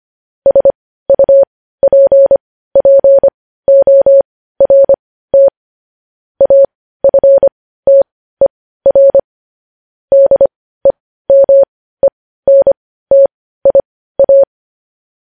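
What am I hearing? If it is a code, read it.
Morse code, heard as a keyed tone, about 25 wpm, spaced as for 15 wpm, 550 Hz. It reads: SUPPORT AFTER DEMENTIA
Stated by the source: Morse code news headlines practice at 15.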